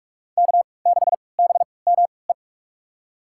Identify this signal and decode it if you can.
Morse code, heard as a keyed tone, about 45 wpm, spaced as for 25 wpm, 700 Hz. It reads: K6BME
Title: USA Call Signs - 5 Characters - 3x Character Spacing - 45wpm